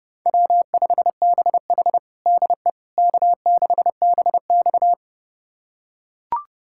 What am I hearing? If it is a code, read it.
Morse code, heard as a keyed tone, about 30 wpm, 700 Hz. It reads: W5BH DE K6BX
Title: USA Callsign DE Callsign - 4 Characters - 30wpm